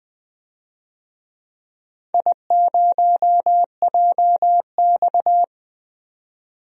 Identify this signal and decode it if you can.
Morse code, heard as a keyed tone, about 20 wpm, 700 Hz. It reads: I0JX
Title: DXCC-100 Call Signs - 4 Characters - 20wpm